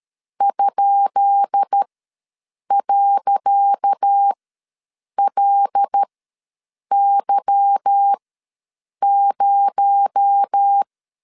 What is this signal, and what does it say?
Morse code, heard as a keyed tone, about 13 wpm, spaced as for 7 wpm, 800 Hz. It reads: ?.LY0
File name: LEZIONI / 10 SIGNS / 10C DOT ws 12wpm 3548